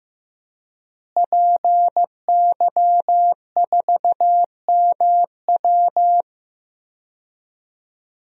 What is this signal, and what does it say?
Morse code, heard as a keyed tone, about 15 wpm, 700 Hz. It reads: PY4MW
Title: DXCC-100 Call Signs - 5 Characters - Repeated 3x - 15wpm